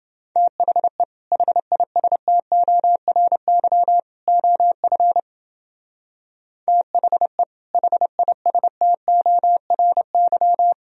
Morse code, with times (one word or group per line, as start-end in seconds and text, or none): 0.4-1.0 THE
1.3-4.0 HISTORY
4.3-5.2 OF
6.7-7.4 THE
7.7-10.7 HISTORY